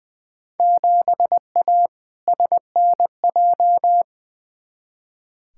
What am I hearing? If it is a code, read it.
Morse code, heard as a keyed tone, about 20 wpm, 700 Hz. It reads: 7A SNJ
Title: Field Day Exchanges Encore - 20wpm